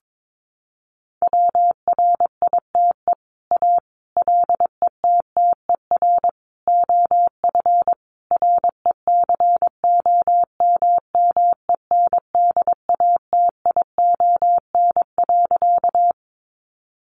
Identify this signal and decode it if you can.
Morse code, heard as a keyed tone, about 22 wpm, 700 Hz. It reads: WRITE A LETTER OF RECOMMENDATION.
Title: Sentences from Top 1400 Words 22wpm